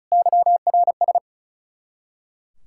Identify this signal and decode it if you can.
Morse code, heard as a keyed tone, about 46 wpm, 700 Hz. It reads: YRS